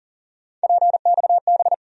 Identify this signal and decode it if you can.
Morse code, heard as a keyed tone, about 38 wpm, 700 Hz. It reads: PXB
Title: USA Call Sign Postfixes - 3 Letters 40wpm